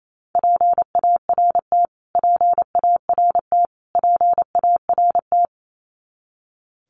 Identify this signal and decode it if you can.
Morse code, heard as a keyed tone, about 27 wpm, 700 Hz. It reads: PART PART PART
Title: Top 500 Words Repeated 28wpm